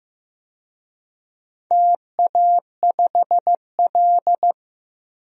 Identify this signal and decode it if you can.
Morse code, heard as a keyed tone, about 16 wpm, 700 Hz. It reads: TA5L
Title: DXCC-100 Call Signs - 4 Characters - Repeated 2x - 15wpm